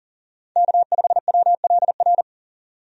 Morse code, heard as a keyed tone, about 40 wpm, 700 Hz. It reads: K5WLR
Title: USA Call Signs - 5 Characters - 40wpm